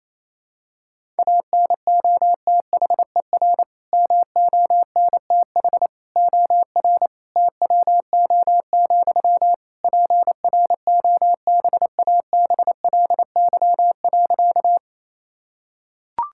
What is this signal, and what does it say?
Morse code, heard as a keyed tone, about 28 wpm, 700 Hz. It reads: ANOTHER MONTH OR TWO, PROBABLY.
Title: Sentences from Top 1000 Words - Review Part 5 - 28wpm